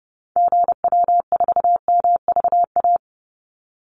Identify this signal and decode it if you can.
Morse code, heard as a keyed tone, about 30 wpm, 700 Hz. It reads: GW4MVA
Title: Call Signs - 30wpm